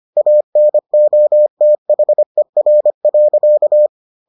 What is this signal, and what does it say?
Morse code, heard as a keyed tone, about 25 wpm, 600 Hz. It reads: ANOTHER.